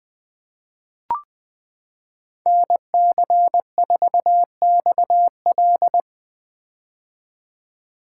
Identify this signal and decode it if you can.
Morse code, heard as a keyed tone, about 20 wpm, 700 Hz. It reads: NC4XL
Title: 2023 POTA Activator Call Signs - 20wpm